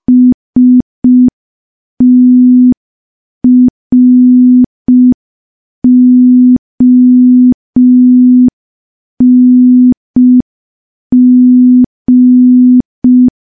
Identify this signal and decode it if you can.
Morse code, heard as a keyed tone, about 5 wpm, 250 Hz. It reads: STRONG
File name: morse / morse-training-05WPM